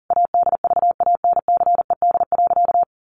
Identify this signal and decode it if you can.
Morse code, heard as a keyed tone, about 40 wpm, 700 Hz. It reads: ADVANCED.
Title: Sentences from Top 1200 Words 40wpm